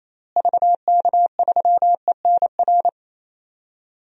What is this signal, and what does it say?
Morse code, heard as a keyed tone, about 28 wpm, 700 Hz. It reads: VK3ENR
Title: DXCC-100 Call Signs - 6 Characters - Repeated 3x - 28wpm